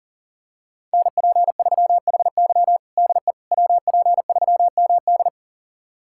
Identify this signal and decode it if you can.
Morse code, heard as a keyed tone, about 40 wpm, 700 Hz. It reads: NP3HY DE WP3MD
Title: USA Callsign DE Callsign - 5 Characters - 40wpm